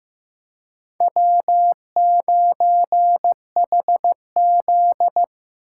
Morse code, heard as a keyed tone, about 15 wpm, 700 Hz. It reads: W9HZ
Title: USA Call Signs - 4 Characters - 4x Character Spacing - 15wpm